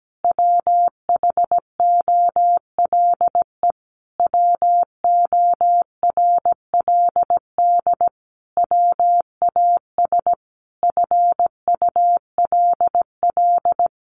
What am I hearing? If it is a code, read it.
Morse code, heard as a keyed tone, about 17 wpm, 700 Hz. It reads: WHOLE WORLD WAS FULL
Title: Sentences from Top 1300 Words 17wpm